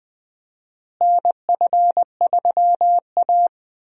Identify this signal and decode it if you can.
Morse code, heard as a keyed tone, about 20 wpm, 700 Hz. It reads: NF3A
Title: USA Call Signs - 4 Characters - Repeated 2x - 20wpm